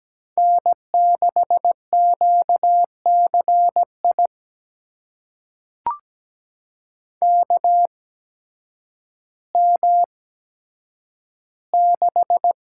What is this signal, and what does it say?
Morse code, heard as a keyed tone, about 17 wpm, spaced as for 9 wpm, 700 Hz. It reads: N6QCI KM6
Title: USA Call Signs - 5 Characters - 8x Character Spacing - 17wpm